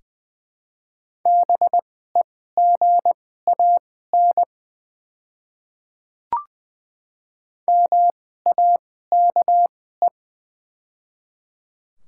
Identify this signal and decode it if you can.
Morse code, heard as a keyed tone, about 20 wpm, spaced as for 14 wpm, 690 Hz. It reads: BEGAN MAKE
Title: 2x Character Spacing - Top 500 Words - 20wpm